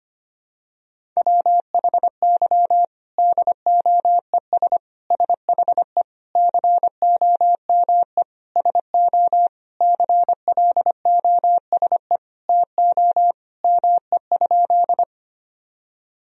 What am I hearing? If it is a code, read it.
Morse code, heard as a keyed tone, about 25 wpm, 700 Hz. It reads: WHY DOES SHE COME SO CLOSE TO ME?
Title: Sentences from Top 1000 Words - Review Part 1 - 25wpm